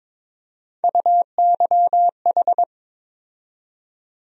Morse code, heard as a keyed tone, about 22 wpm, 700 Hz. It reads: UYH
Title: USA Call Sign Postfixes - 3 Letters 22wpm